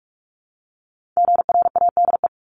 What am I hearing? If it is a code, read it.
Morse code, heard as a keyed tone, about 46 wpm, 700 Hz. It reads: GRADE
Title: Multi-letter Phonemes Repeated 3x 45wpm